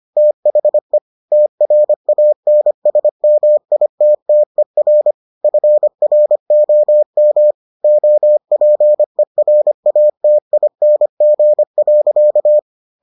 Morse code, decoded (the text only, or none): THE TRANSMITTER FROM OPERATING.